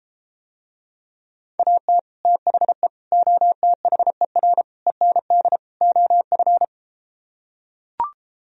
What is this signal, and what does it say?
Morse code, heard as a keyed tone, about 33 wpm, 700 Hz. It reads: AT THE OTHER END OF